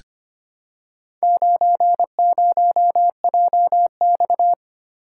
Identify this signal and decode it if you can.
Morse code, heard as a keyed tone, about 25 wpm, 700 Hz. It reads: 90JX
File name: practice / 25wpm / Four Letter-Number Combinations - 25wpm